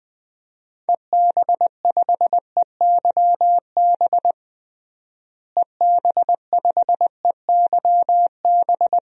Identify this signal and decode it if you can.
Morse code, heard as a keyed tone, about 20 wpm, 700 Hz. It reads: EB5EYB EB5EYB
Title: DXCC-100 Call Signs - 6 Characters - Repeated 2x - 20wpm